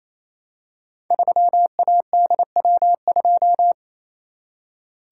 Morse code, heard as a keyed tone, about 27 wpm, 700 Hz. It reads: 3ADW2
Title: Code Groups - 5 Letter:Numbers - 28wpm